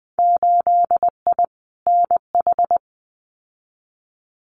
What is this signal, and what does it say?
Morse code, heard as a keyed tone, about 20 wpm, 700 Hz. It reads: 8I NH
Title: Winter Field Day Exchanges Encore 20wpm